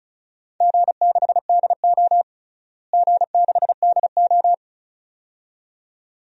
Morse code, heard as a keyed tone, about 34 wpm, 700 Hz. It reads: G6DO G6DO
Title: DXCC-100 Call Signs - 4 Characters - Repeated 2x - 35wpm